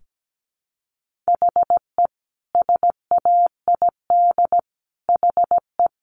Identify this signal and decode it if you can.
Morse code, heard as a keyed tone, about 17 wpm, 700 Hz. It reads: HE SAID HE
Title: Sentences from Top 1500 Words 17wpm